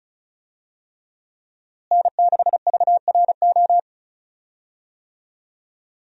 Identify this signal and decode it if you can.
Morse code, heard as a keyed tone, about 36 wpm, 700 Hz. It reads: N6VRO
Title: USA Call Signs - 5 Characters - Repeated 3x - 35wpm